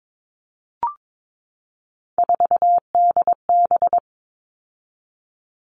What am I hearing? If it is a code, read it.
Morse code, heard as a keyed tone, about 22 wpm, 700 Hz. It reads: 4DB
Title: All Cut Numbers - 3-Digit 22wpm